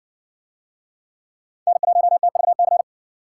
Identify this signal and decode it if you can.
Morse code, heard as a keyed tone, about 60 wpm, 700 Hz. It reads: N1TVC